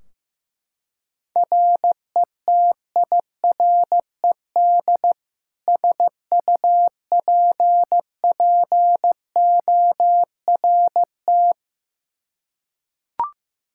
Morse code, heard as a keyed tone, about 15 wpm, 700 Hz. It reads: RETIRED SUPPORT
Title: Practice QSO - Exchange 3 - More Info - 15wpm